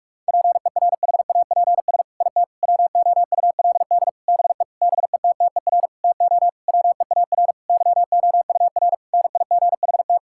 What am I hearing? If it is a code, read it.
Morse code, heard as a keyed tone, about 46 wpm, 700 Hz. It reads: PERHAPS IT WOULD BE BETTER TO WEAR YOUR NIGHT
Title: Sentences from Top 1600 Words 45wpm